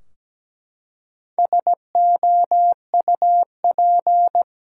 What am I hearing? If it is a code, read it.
Morse code, heard as a keyed tone, about 17 wpm, 700 Hz. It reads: SOUP
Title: Four Letter Words - 17wpm